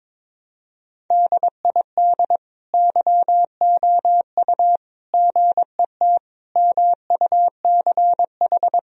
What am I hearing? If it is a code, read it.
Morse code, heard as a keyed tone, about 22 wpm, 690 Hz. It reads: DID YOU GET MUCH
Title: Sentences from Top 1000 Words - Review Part 3 - 22wpm